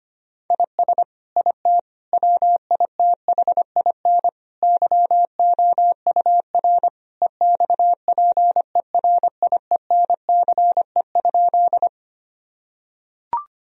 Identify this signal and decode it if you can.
Morse code, heard as a keyed tone, about 25 wpm, 700 Hz. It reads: IS IT WITHIN YOUR EXPERIENCE?
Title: Sentences from Top 1000 Words - Review Part 4 - 25wpm